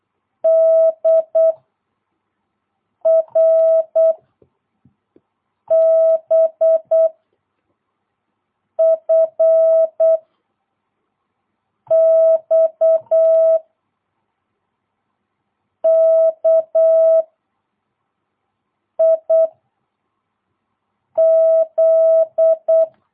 Morse code, heard as a keyed tone, about 8 wpm, 650 Hz. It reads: D R B F X K I Z
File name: jh8cbh / 2022cw